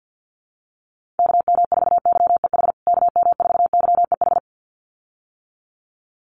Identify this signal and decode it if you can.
Morse code, heard as a keyed tone, about 50 wpm, 700 Hz. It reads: KM4YEH KM4YEH
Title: USA Call Signs Repeated 2x 50wpm